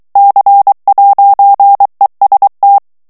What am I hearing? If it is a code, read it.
Morse code, heard as a keyed tone, about 23 wpm, 800 Hz. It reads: C'EST
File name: FRANCAIS TEXTE CLAIR 1 VITESSE 22 MOTS MINUTES